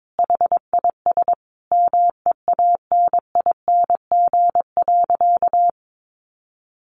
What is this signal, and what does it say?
Morse code, heard as a keyed tone, about 22 wpm, 700 Hz. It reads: HIS MEANING.
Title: Sentences from Top 1100 Words 22wpm